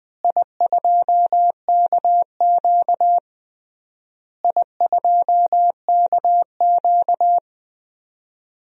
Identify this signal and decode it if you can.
Morse code, heard as a keyed tone, about 20 wpm, 700 Hz. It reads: I2KQ I2KQ
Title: DXCC-100 Call Signs - 4 Characters - Repeated 3x - 20wpm